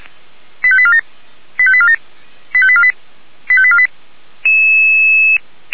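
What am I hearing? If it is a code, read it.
Five-tone selective call (ZVEI), heard as quick runs of five tones, tones of about 70 ms. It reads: 85847 85847 85848 85848
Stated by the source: Five-tone ZVEI selective call received by SDR, fire brigade Orlamünde (feuerwehr-orlamuende), code 89494.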